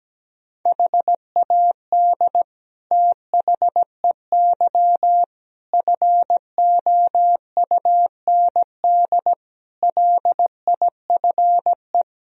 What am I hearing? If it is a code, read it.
Morse code, heard as a keyed tone, about 17 wpm, 700 Hz. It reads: HAD THEY FOUND LIFE
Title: Sentences from Top 1000 Words - Review Part 2 - 17wpm